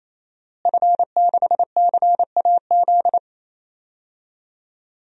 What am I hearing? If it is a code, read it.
Morse code, heard as a keyed tone, about 28 wpm, 700 Hz. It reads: F6CAZ
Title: DXCC-100 Call Signs - 5 Characters - Repeated 3x - 28wpm